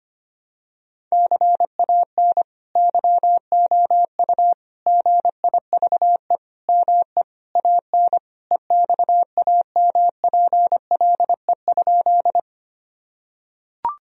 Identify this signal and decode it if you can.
Morse code, heard as a keyed tone, about 25 wpm, 700 Hz. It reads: CAN YOU GIVE ME AN EXAMPLE?
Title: Sentences from Top 1000 Words - Review Part 1 - 25wpm